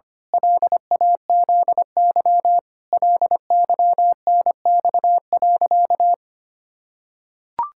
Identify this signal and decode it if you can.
Morse code, heard as a keyed tone, about 25 wpm, 700 Hz. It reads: LAZY LYNX.
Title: Pangram Challenge v2 - 25wpm